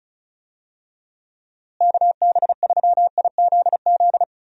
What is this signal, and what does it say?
Morse code, heard as a keyed tone, about 36 wpm, 700 Hz. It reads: KB3IZZ
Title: USA Call Signs - 6 Characters - Repeated 2x - 35wpm